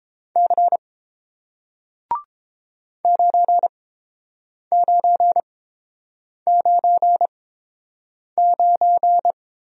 Morse code, heard as a keyed tone, about 30 wpm, spaced as for 7 wpm, 700 Hz. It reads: C 9999